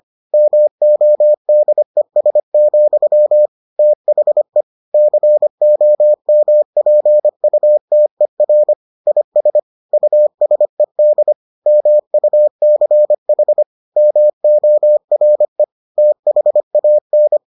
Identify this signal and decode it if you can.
Morse code, heard as a keyed tone, about 25 wpm, 600 Hz. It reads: MODES, THE COMPUTER IS USED MUCH MORE THAN